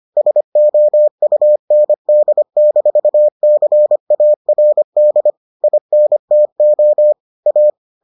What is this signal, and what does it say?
Morse code, heard as a keyed tone, about 25 wpm, 600 Hz. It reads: SOUND-CARD INTO A